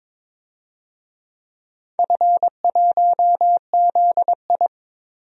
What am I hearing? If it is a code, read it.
Morse code, heard as a keyed tone, about 22 wpm, 700 Hz. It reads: F1ZI
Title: DXCC-100 Call Signs - 4 Characters - 22wpm